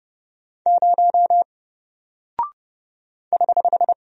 Morse code, heard as a keyed tone, about 30 wpm, 700 Hz. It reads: 0 <HH>
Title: Single Character (Letter:Number:Punctuation:Prosign) Alt 30wpm